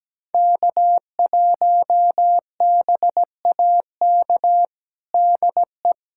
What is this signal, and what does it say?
Morse code, heard as a keyed tone, about 17 wpm, 700 Hz. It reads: K1BAK DE
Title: USA Callsign DE Callsign - 5 Characters - 17wpm